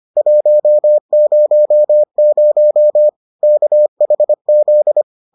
Morse code, heard as a keyed tone, about 25 wpm, 600 Hz. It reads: 100 KHZ